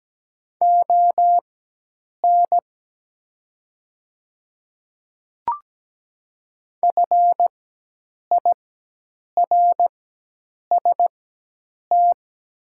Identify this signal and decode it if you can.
Morse code, heard as a keyed tone, about 17 wpm, spaced as for 8 wpm, 700 Hz. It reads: ON FIRST
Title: 4x Character Spacing - Top 100 Words - 17wpm